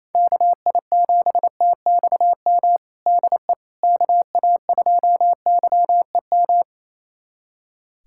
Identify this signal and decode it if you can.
Morse code, heard as a keyed tone, about 28 wpm, 700 Hz. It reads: KI7TXM DE KA2YEM
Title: USA Callsign DE Callsign - 6 Characters - 28wpm